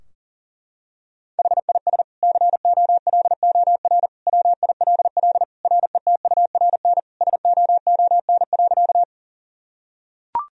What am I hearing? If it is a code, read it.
Morse code, heard as a keyed tone, about 40 wpm, 700 Hz. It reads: HIS COLOR WILL RETURN SOON.
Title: Sentences from Top 1000 Words - Review Part 2 - 40wpm